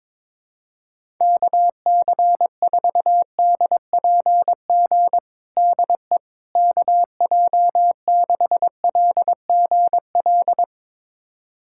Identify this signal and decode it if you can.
Morse code, heard as a keyed tone, about 22 wpm, 700 Hz. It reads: KC4DPG DE KJ6LGL
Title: USA Callsign DE Callsign - 6 Characters - 22wpm